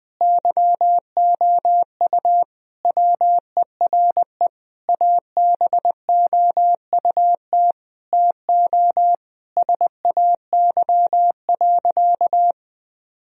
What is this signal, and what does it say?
Morse code, heard as a keyed tone, about 20 wpm, 700 Hz. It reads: YOU WERE ABOUT TO SAY.
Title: Sentences from Top 1000 Words - Review Part 2 - 20wpm